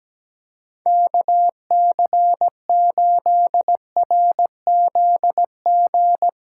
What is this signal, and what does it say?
Morse code, heard as a keyed tone, about 17 wpm, 700 Hz. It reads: KC8RZG
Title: USA Callsign DE Callsign 17wpm